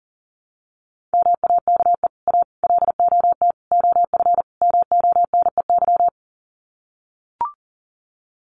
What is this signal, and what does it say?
Morse code, heard as a keyed tone, about 40 wpm, 700 Hz. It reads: MAKE A LOT OF MONEY
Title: Sets of 5 Words Encore2 40wpm